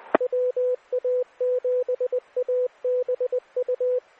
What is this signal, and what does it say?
Morse code, heard as a keyed tone, about 20 wpm, 500 Hz. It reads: WA7ABU
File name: WA7ABU / 529PrepNet